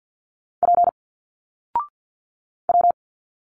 Morse code, heard as a keyed tone, about 52 wpm, 700 Hz. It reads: P W
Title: Single Letters 50wpm